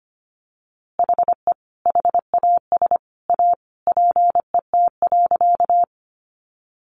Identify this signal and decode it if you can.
Morse code, heard as a keyed tone, about 25 wpm, 700 Hz. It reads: HE HAS A PET.
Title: Easy Phrases 25wpm